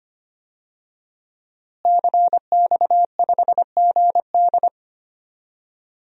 Morse code, heard as a keyed tone, about 25 wpm, 700 Hz. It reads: CX5GD
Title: DXCC-100 Call Signs - 5 Characters - Repeated 2x - 25wpm